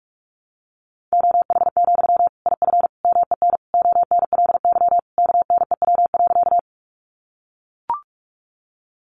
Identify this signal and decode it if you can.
Morse code, heard as a keyed tone, about 44 wpm, 700 Hz. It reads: OH, IF MEN ONLY KNEW.
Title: Sentences from Top 500 Words 45wpm